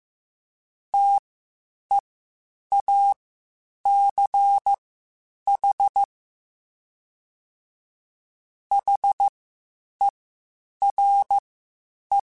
Morse code, heard as a keyed tone, about 15 wpm, spaced as for 8 wpm, 800 Hz. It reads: TEACH HERE
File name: (spaced) / QSOs Spaced-mp3s / WD0BGP 1 @13 sp